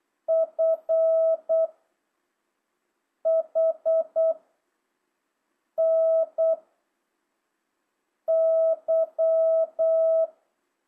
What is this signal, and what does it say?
Morse code, heard as a keyed tone, about 8 wpm, 660 Hz. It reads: F H N Y
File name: jh8cbh / 2025cw